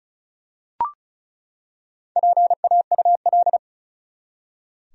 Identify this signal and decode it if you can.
Morse code, heard as a keyed tone, about 36 wpm, 700 Hz. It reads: PAUL
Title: USA Common Names - Max 4 Letters - 35wpm